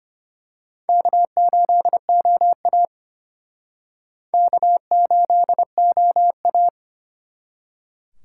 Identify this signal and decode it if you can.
Morse code, heard as a keyed tone, about 29 wpm, 700 Hz. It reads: K8OA K8OA
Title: Speed Racing - USA 1x2 Call Signs 25wpm